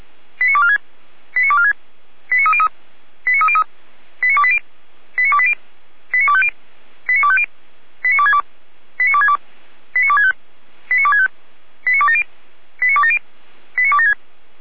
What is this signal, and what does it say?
Five-tone selective call (ZVEI), heard as quick runs of five tones, tones of about 70 ms. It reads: 89246 89246 89303 89303 89280 89280 89250 89250 89262 89262 89265 89265 89270 89270 89276
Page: /455/SDR/ZVEI/89174/